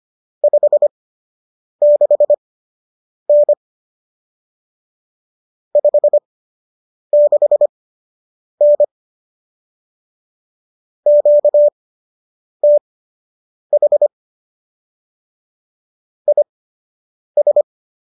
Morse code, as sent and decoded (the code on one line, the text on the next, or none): ..... -.... -. / ..... -.... -. / --.- - .... / .. ...
56N 56N QTH IS